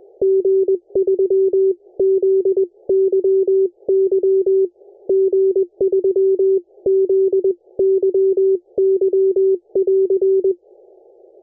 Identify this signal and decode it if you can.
Morse code, heard as a keyed tone, about 21 wpm, spaced as for 18 wpm, 380 Hz. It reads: G3ZYY G3ZYY<AR>
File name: oa4tt-bcg-zyy